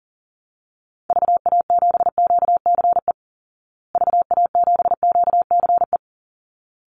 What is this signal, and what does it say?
Morse code, heard as a keyed tone, about 40 wpm, 700 Hz. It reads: VA7QCE VA7QCE
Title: Call Signs Repeated 3x - 40wpm